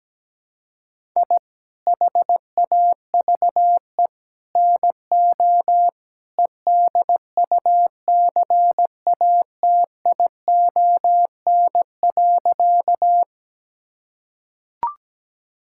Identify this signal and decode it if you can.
Morse code, heard as a keyed tone, about 17 wpm, 700 Hz. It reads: I HAVE NO EDUCATION.
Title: Sentences from Top 1000 Words - Review Part 6 - 17wpm